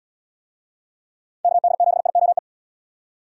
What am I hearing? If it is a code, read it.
Morse code, heard as a keyed tone, about 71 wpm, 700 Hz. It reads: KD7IOE